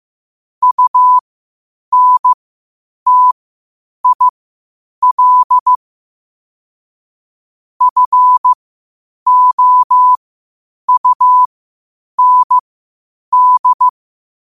Morse code, heard as a keyed tone, about 15 wpm, spaced as for 8 wpm, 1000 Hz. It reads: UNTIL FOUND
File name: qotdfiles / 8wpm